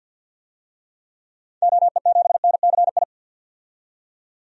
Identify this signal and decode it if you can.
Morse code, heard as a keyed tone, about 51 wpm, 700 Hz. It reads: OE7NCI
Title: DXCC-100 Call Signs - 6 Characters - 50wpm